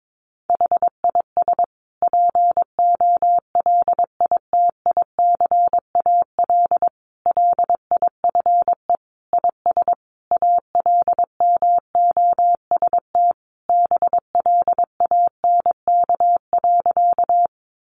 Morse code, heard as a keyed tone, about 22 wpm, 700 Hz. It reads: HIS POLITICAL LIFE IS ALMOST BLANK.
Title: Sentences from Top 1500 Words 22wpm